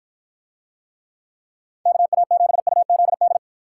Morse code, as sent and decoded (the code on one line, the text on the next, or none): -.- .- --... ..- --.. -..
KA7UZD